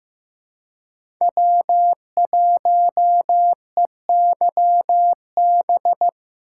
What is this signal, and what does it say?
Morse code, heard as a keyed tone, about 15 wpm, 700 Hz. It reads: W1EYB
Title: USA Call Signs - 5 Characters - 3x Character Spacing - 15wpm